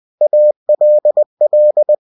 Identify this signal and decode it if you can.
Morse code, heard as a keyed tone, about 20 wpm, 600 Hz. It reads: ALL
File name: cw-books / winnie-the-pooh / winnie-20wpm-page-0071